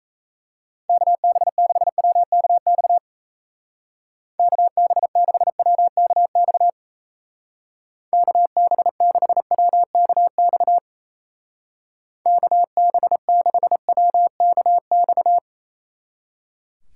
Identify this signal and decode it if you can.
Morse code, heard as a keyed tone, about 35 wpm, 700 Hz. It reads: KB6WKX KB6WKX KB6WKX KB6WKX
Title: Speed Racing - USA 2x3 Call Signs 28wpm